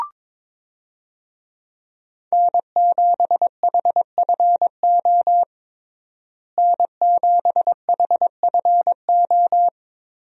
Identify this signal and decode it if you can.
Morse code, heard as a keyed tone, about 22 wpm, 700 Hz. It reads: N7HFO N7HFO
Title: USA Call Signs - 5 Characters - Repeated 3x - 22wpm